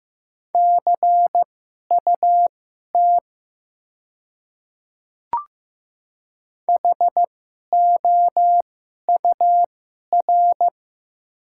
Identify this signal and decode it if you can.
Morse code, heard as a keyed tone, about 15 wpm, spaced as for 11 wpm, 700 Hz. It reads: CUT HOUR
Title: 2x Character Spacing - Top 500 Words - 15wpm